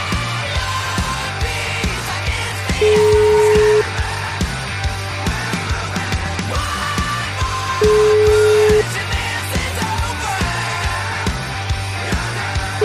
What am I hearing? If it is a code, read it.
Ringback tone (Europe, 425 Hz), with noise.